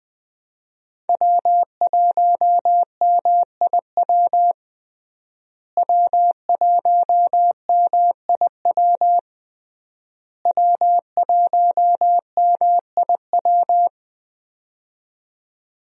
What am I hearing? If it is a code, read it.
Morse code, heard as a keyed tone, about 20 wpm, 700 Hz. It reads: W1MIW W1MIW W1MIW
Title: USA Call Signs - 5 Characters - Repeated 3x - 20wpm